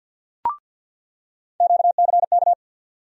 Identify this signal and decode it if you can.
Morse code, heard as a keyed tone, about 51 wpm, 700 Hz. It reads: QCK
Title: USA Call Sign Postfixes - 3 Letters 50wpm